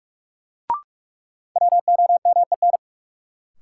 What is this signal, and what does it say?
Morse code, heard as a keyed tone, about 44 wpm, 700 Hz. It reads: WOMEN
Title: Sentences from Top 1300 Words - New Word Drill - 45wpm